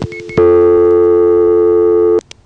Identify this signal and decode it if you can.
Dial tone (North America), 350 + 440 Hz.